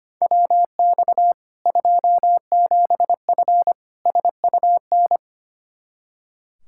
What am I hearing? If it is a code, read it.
Morse code, heard as a keyed tone, about 25 wpm, 700 Hz. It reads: WX 27F SUN
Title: Practice QSO - Exchange 2 - Info - 25wpm